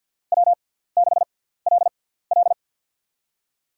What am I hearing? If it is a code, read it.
Morse code, heard as a keyed tone, about 53 wpm, spaced as for 17 wpm, 700 Hz. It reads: W6LL
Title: USA Call Signs - 6x Character Spacing - 50wpm